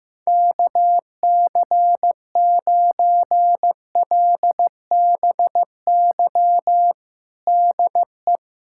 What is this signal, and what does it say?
Morse code, heard as a keyed tone, about 15 wpm, 700 Hz. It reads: KC9LBY DE
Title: USA Callsign DE Callsign - 6 Characters - 15wpm